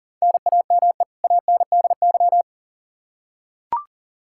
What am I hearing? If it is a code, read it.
Morse code, heard as a keyed tone, about 37 wpm, 700 Hz. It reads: NAME ANDY
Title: Practice QSO - Exchange 1 - Beginning - 40wpm